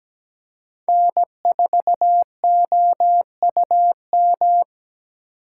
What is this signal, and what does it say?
Morse code, heard as a keyed tone, about 17 wpm, 700 Hz. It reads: N4OUM
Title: USA Call Signs - 3x Character Spacing - 17wpm